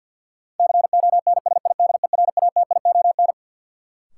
Morse code, heard as a keyed tone, about 48 wpm, 700 Hz. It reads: CONSIDERATION
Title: Sentences from Top 1500 Words - New Word Drill - 50wpm